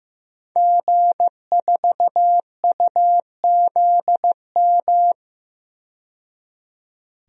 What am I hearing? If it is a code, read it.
Morse code, heard as a keyed tone, about 15 wpm, 700 Hz. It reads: G4UZM
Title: DXCC-100 Call Signs - 5 Characters - 15wpm